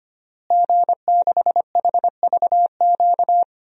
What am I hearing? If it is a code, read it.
Morse code, heard as a keyed tone, about 25 wpm, 700 Hz. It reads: G6HVQ
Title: DXCC-100 Call Signs - 5 Characters - Repeated 3x - 25wpm